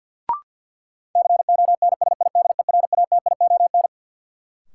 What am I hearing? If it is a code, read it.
Morse code, heard as a keyed tone, about 53 wpm, 700 Hz. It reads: CONSIDERATION
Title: Sentences from Top 1500 Words - New Word Drill - 50wpm